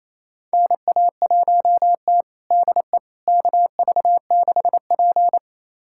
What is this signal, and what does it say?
Morse code, heard as a keyed tone, about 27 wpm, 690 Hz. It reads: NA1T DE KV6P
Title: USA Callsign DE Callsign - 4 Characters - 28wpm